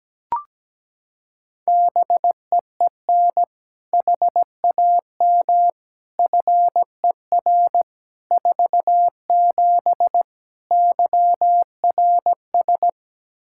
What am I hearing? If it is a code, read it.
Morse code, heard as a keyed tone, about 17 wpm, 700 Hz. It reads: BEEN HAM FER 47 YRS
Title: Practice QSO - Exchange 3 - More Info - 17wpm